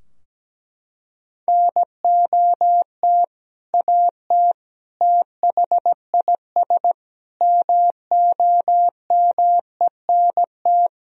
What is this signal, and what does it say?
Morse code, heard as a keyed tone, about 17 wpm, 700 Hz. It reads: NOT AT THIS MOMENT
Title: Sentences from Top 1000 Words - Review Part 4 - 17wpm